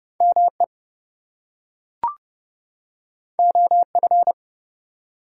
Morse code, heard as a keyed tone, about 30 wpm, 700 Hz. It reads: ME OF
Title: Two Letter Words - 30wpm